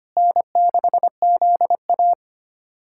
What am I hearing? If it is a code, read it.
Morse code, heard as a keyed tone, about 25 wpm, 700 Hz. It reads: N6ZA
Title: USA Callsign DE Callsign - 4 Characters - 25wpm